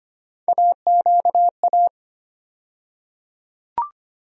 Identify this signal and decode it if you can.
Morse code, heard as a keyed tone, about 25 wpm, 700 Hz. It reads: AQA
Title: USA Call Sign Postfixes - 3 Letters 25wpm